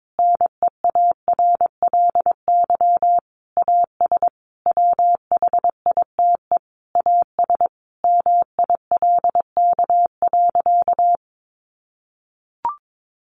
Morse code, heard as a keyed tone, about 22 wpm, 700 Hz. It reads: NEARLY AS WHITE AS MILK.
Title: Sentences from Top 1500 Words 22wpm